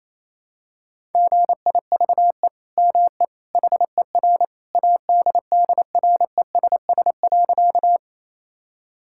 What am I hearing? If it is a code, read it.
Morse code, heard as a keyed tone, about 28 wpm, 700 Hz. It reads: GIVE ME HER ADDRESS.
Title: Sentences from Top 1000 Words - Review Part 6 - 28wpm